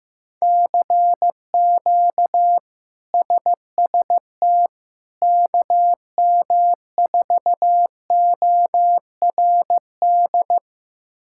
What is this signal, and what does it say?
Morse code, heard as a keyed tone, about 15 wpm, 700 Hz. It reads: CQ SST KM4ORD
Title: K1USN SST QSO Practice - 15wpm